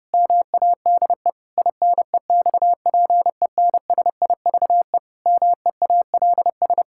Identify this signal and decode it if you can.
Morse code, heard as a keyed tone, about 30 wpm, 700 Hz. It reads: MADE INEXPENSIVE MEALS
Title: Pangram Challenge v2 - 30wpm